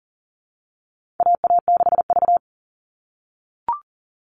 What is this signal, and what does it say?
Morse code, heard as a keyed tone, about 40 wpm, 700 Hz. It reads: AA6V
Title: All Cut Numbers - 4-Digit 40wpm